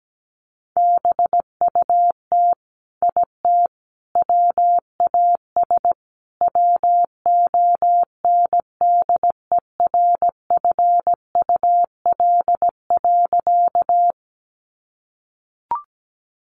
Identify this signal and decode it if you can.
Morse code, heard as a keyed tone, about 17 wpm, 700 Hz. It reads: BUT IT WAS WONDERFUL.
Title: Sentences from Top 1100 Words 17wpm